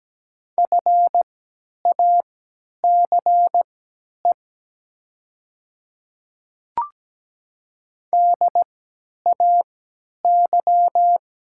Morse code, heard as a keyed tone, about 17 wpm, spaced as for 10 wpm, 700 Hz. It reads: FACE DAY
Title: 3x Character Spacing - Top 500 Words - 17wpm